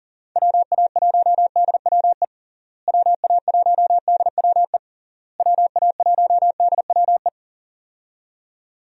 Morse code, heard as a keyed tone, about 37 wpm, 700 Hz. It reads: WA1DWE WA1DWE WA1DWE
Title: USA Call Signs - 6 Characters - Repeated 3x - 40wpm